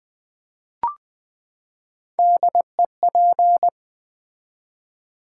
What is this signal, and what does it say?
Morse code, heard as a keyed tone, about 20 wpm, 700 Hz. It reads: DEP
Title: Build A Word - 3 Letters At A Time - 6-Letter Words - 20wpm